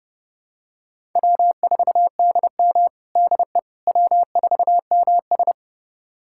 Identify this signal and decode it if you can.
Morse code, heard as a keyed tone, about 30 wpm, 700 Hz. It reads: W4DM DE W4MS
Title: USA Callsign DE Callsign - 4 Characters - 30wpm